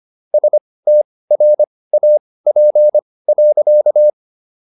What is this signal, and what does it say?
Morse code, heard as a keyed tone, about 25 wpm, spaced as for 18 wpm, 600 Hz. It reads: STRAP.